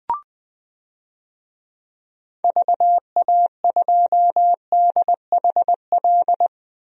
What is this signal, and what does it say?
Morse code, heard as a keyed tone, about 20 wpm, 700 Hz. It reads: VA2DHL
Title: DXCC-100 Call Signs - 6 Characters - 20wpm